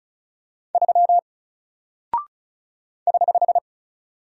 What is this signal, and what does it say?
Morse code, heard as a keyed tone, about 36 wpm, 700 Hz. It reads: 3 <HH>
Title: Single Character (Letter:Number:Punctuation:Prosign) Alt 35wpm